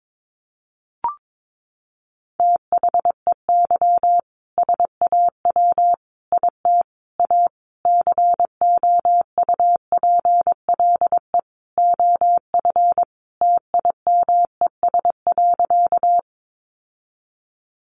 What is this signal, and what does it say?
Morse code, heard as a keyed tone, about 22 wpm, 700 Hz. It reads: THEY SAW IT A COUPLE OF TIMES.
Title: Sentences from Top 1100 Words 22wpm